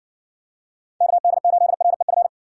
Morse code, heard as a keyed tone, about 57 wpm, 700 Hz. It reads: KD8REF